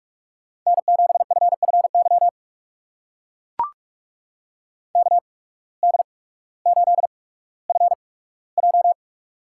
Morse code, heard as a keyed tone, about 44 wpm, spaced as for 19 wpm, 700 Hz. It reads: N7FFY KD8FJ
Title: USA Call Signs - 8x Character Spacing - 45wpm